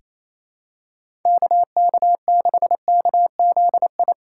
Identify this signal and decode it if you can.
Morse code, heard as a keyed tone, about 28 wpm, 700 Hz. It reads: KK6KZI
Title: USA Call Signs - 6 Characters - 3x Character Spacing - 28wpm